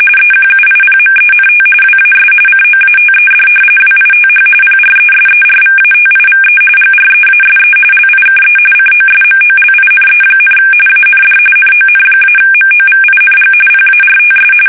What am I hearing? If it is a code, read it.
RTTY, about 45.45 baud. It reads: EQUIPMENT WAS COATED WITH PRESERVATIVE GREASE LIKE COSMOLINE, AND NON-MOVING PARTS
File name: fsk850 / battleshipnj